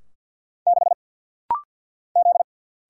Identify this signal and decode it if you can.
Morse code, heard as a keyed tone, about 50 wpm, 700 Hz. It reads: 6 Z
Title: Single Letter-Number Alt - Rapid-Fire 50wpm